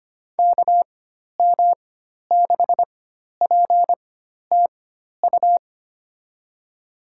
Morse code, heard as a keyed tone, about 25 wpm, spaced as for 12 wpm, 700 Hz. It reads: KM6PTU
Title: USA Call Signs - 6 Characters - 4x Character Spacing - 25wpm